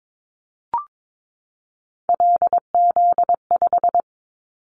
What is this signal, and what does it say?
Morse code, heard as a keyed tone, about 22 wpm, 700 Hz. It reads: LZ5
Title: Callsign Prefixes for Top 100 DXCC - 22wpm